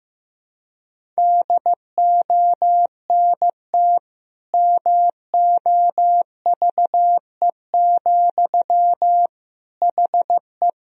Morse code, heard as a keyed tone, about 15 wpm, 700 Hz. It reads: DONT MOVE, HE
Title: Sentences from Top 1000 Words - Review Part 1 - 15wpm